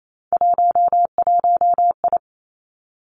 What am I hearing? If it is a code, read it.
Morse code, heard as a keyed tone, about 28 wpm, 700 Hz. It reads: 11I
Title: Three Challenging Letter-Number Combinations 28wpm